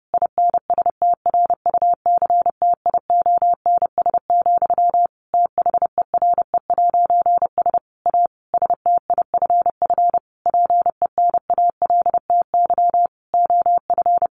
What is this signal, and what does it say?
Morse code, heard as a keyed tone, about 30 wpm, 700 Hz. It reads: INSTRUCTIONS, THERE'S A STIFF PENALTY OF
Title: A Sound Of Thunder 30wpm